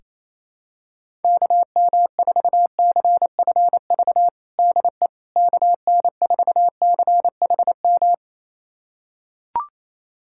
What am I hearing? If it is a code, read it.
Morse code, heard as a keyed tone, about 28 wpm, 700 Hz. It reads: KM4CFV DE KN4CHM
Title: USA Callsign DE Callsign - 6 Characters - 28wpm